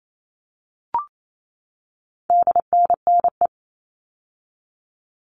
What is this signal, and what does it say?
Morse code, heard as a keyed tone, about 28 wpm, 700 Hz. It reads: DNNE
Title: All Cut Numbers - 4-Digit 28wpm